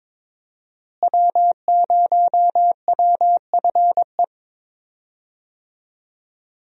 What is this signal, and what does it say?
Morse code, heard as a keyed tone, about 22 wpm, 700 Hz. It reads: W0WFE
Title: USA Call Signs - 5 Characters - Repeated 3x - 22wpm